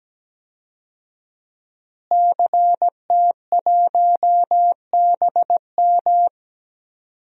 Morse code, heard as a keyed tone, about 17 wpm, 700 Hz. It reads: CT1BM
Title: DXCC-100 Call Signs - 5 Characters - Repeated 2x - 17wpm